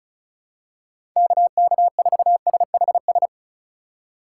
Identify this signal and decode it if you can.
Morse code, heard as a keyed tone, about 34 wpm, 700 Hz. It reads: KK4SHS
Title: USA Call Signs - 6 Characters - 4x Character Spacing - 35wpm